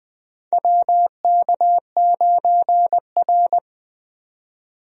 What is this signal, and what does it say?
Morse code, heard as a keyed tone, about 20 wpm, 700 Hz. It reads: WK9R
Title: USA Call Signs - 4 Characters - 20wpm